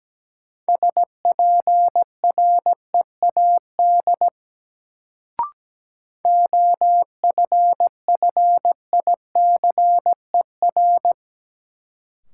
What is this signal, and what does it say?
Morse code, heard as a keyed tone, about 17 wpm, 700 Hz. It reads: SPREAD OFFICER
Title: Sentences from Top 1200 Words - New Word Drill - 17wpm